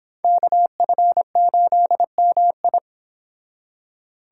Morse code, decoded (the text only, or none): KF8MI